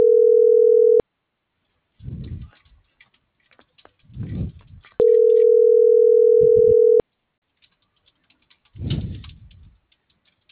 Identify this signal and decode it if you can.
Ringback tone (North America), 440 + 480 Hz.